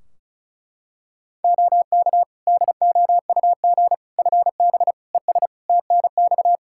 Morse code, heard as a keyed tone, about 36 wpm, 700 Hz. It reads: OK DOUG FB ES TNX
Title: Practice QSO - Exchange 2 - Info - 35wpm